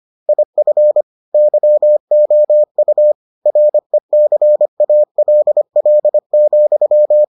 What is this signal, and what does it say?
Morse code, heard as a keyed tone, about 25 wpm, 600 Hz. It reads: IF YOU RECALL,